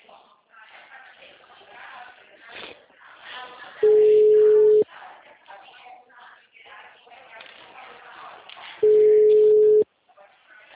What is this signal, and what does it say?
Ringback tone (Europe, 425 Hz), with speech.
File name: out- -2009- - - .15470